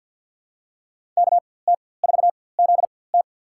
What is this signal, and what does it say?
Morse code, heard as a keyed tone, about 52 wpm, spaced as for 23 wpm, 700 Hz. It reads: KT4ZT
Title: USA Call Signs - 5 Characters - 4x Character Spacing - 50wpm